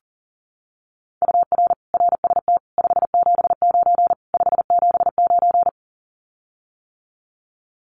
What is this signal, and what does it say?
Morse code, heard as a keyed tone, about 40 wpm, 700 Hz. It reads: UR RST 579 579
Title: Common QSO Phrases Encore 40wpm